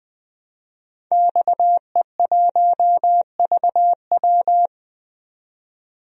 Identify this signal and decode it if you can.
Morse code, heard as a keyed tone, about 20 wpm, 700 Hz. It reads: XE1VW
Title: DXCC-100 Call Signs - 5 Characters - 20wpm